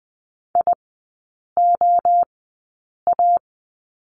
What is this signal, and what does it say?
Morse code, heard as a keyed tone, about 20 wpm, 700 Hz. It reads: I O A
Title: 029 Lesson 6e - Letter i - Three Characters 20wpm